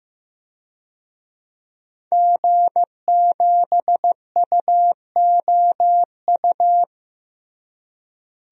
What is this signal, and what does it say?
Morse code, heard as a keyed tone, about 15 wpm, 700 Hz. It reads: G7UOU
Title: DXCC-100 Call Signs - 5 Characters - Repeated 3x - 15wpm